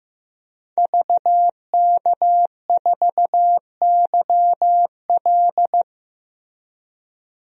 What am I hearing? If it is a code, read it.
Morse code, heard as a keyed tone, about 15 wpm, 700 Hz. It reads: VK4YL
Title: DXCC-100 Call Signs - 5 Characters - 15wpm